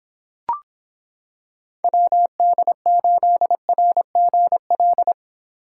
Morse code, heard as a keyed tone, about 26 wpm, 700 Hz. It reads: WD8RGL